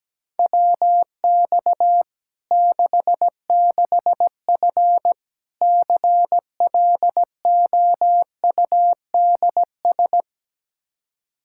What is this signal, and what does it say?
Morse code, heard as a keyed tone, about 17 wpm, 700 Hz. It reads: WX 66F CLOUDS
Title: Practice QSO - Exchange 2 - Info - 17wpm